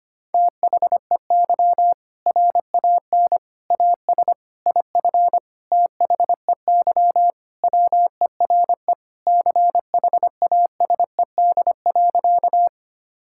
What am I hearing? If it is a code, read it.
Morse code, heard as a keyed tone, about 25 wpm, 700 Hz. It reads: THEY RAN AS IF THEY WERE CHASED.
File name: practice / 25wpm / Sentences with 4-Word Phrases - 25wpm